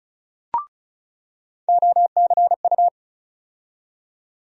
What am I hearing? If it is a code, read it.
Morse code, heard as a keyed tone, about 36 wpm, 700 Hz. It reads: OCU
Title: USA Call Sign Postfixes - 3 Letters 35wpm